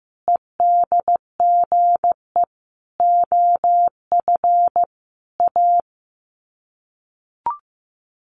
Morse code, heard as a keyed tone, about 15 wpm, 700 Hz. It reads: EDGE OF A
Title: Sets of 5 Words Encore2 15wpm